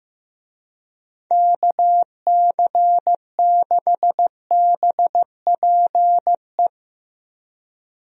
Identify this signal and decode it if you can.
Morse code, heard as a keyed tone, about 15 wpm, 700 Hz. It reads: KC6BPE
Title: USA Call Signs - 6 Characters - Repeated 2x - 15wpm